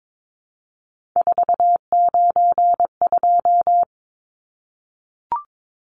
Morse code, heard as a keyed tone, about 22 wpm, 700 Hz. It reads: 492
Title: Numbers - 3-Digit 22wpm